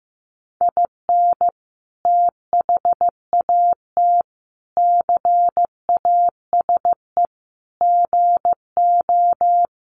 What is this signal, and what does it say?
Morse code, heard as a keyed tone, about 14 wpm, 700 Hz. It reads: IN THAT CASE GO
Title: Sentences from Top 700 Words 15wpm